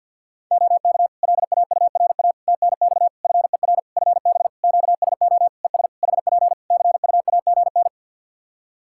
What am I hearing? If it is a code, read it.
Morse code, heard as a keyed tone, about 49 wpm, 700 Hz. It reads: OK LAURA TNX FER FB QSO ES HP CUAGN
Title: K1USN SST QSO Practice 50wpm